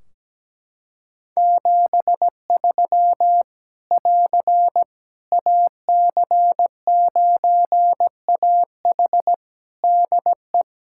Morse code, heard as a keyed tone, about 17 wpm, 700 Hz. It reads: 73 <AR> AC9AH DE
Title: Practice QSO - Exchange 4 - Ending - 17wpm